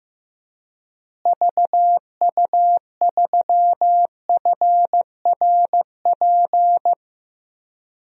Morse code, heard as a keyed tone, about 15 wpm, 700 Hz. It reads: VU3FRP
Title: DXCC-100 Call Signs - 6 Characters - Repeated 3x - 15wpm